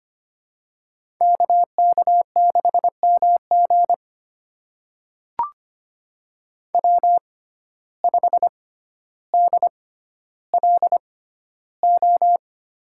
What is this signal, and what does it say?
Morse code, heard as a keyed tone, about 25 wpm, spaced as for 13 wpm, 700 Hz. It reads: KK6MG W5DLO
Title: USA Call Signs - 5 Characters - 6x Character Spacing - 25wpm